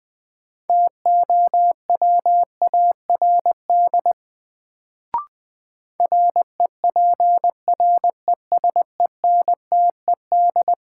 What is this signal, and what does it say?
Morse code, heard as a keyed tone, about 20 wpm, 700 Hz. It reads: TOWARD REPRESENTED
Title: Sets of 5 Words - New Word Drill - 20wpm